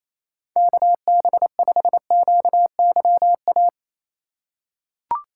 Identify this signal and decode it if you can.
Morse code, heard as a keyed tone, about 28 wpm, 700 Hz. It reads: KB5QYA
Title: USA Call Signs - 2x Character Spacing - 28wpm